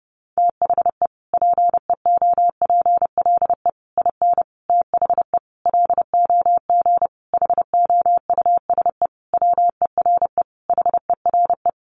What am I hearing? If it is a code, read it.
Morse code, heard as a keyed tone, about 30 wpm, 700 Hz. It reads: THE PEOPLE IN THE LOG HOUSE WERE HERE
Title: Sentences from Top 1400 Words 30wpm